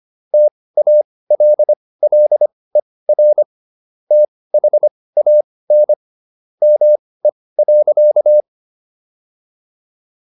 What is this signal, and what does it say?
Morse code, heard as a keyed tone, about 25 wpm, spaced as for 18 wpm, 600 Hz. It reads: TALLER THAN ME.
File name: SS123 18